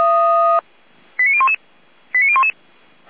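Five-tone selective call (ZVEI), heard as quick runs of five tones, tones of about 70 ms. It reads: 89011 89011
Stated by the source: ZVEI five-tone call 89001 received by SDR, recorded by Feuerwehr Orlamünde.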